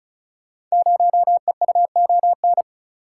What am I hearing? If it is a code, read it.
Morse code, heard as a keyed tone, about 36 wpm, 700 Hz. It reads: 0EUON
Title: Code Groups - 5 Letter:Numbers - 35wpm